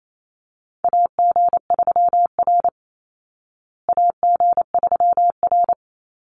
Morse code, heard as a keyed tone, about 28 wpm, 700 Hz. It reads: AG3R AG3R
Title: Call Signs Repeated 3x - 28wpm